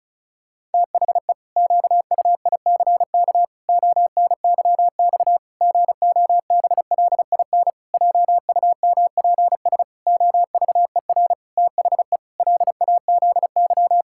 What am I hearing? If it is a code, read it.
Morse code, heard as a keyed tone, about 36 wpm, 700 Hz. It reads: THE QUICK ONYX GOBLIN JUMPS OVER THE LAZY
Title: Pangram Challenge v2 - 35wpm